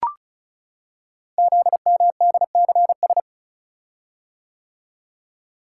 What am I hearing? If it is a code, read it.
Morse code, heard as a keyed tone, about 36 wpm, 690 Hz. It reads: ZMDCS
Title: Code Groups - 5 Letters - 35wpm